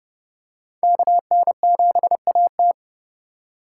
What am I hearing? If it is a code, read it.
Morse code, heard as a keyed tone, about 30 wpm, 700 Hz. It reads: KN7AT
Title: USA Call Signs - 5 Characters - 4x Character Spacing - 30wpm